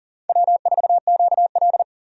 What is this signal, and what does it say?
Morse code, heard as a keyed tone, about 37 wpm, 700 Hz. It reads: W4QL
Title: USA Call Signs - 4 Characters - 40wpm